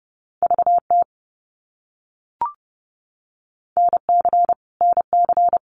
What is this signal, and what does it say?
Morse code, heard as a keyed tone, about 29 wpm, 700 Hz. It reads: VT NC NC
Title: US States Repeated 3x 30wpm